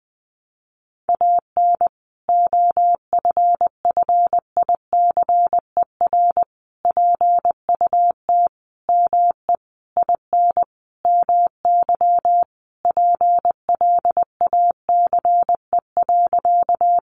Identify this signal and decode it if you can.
Morse code, heard as a keyed tone, about 20 wpm, 700 Hz. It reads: AN OFFICER PUT ME IN MY PLACE.
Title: Sentences from Top 1200 Words 20wpm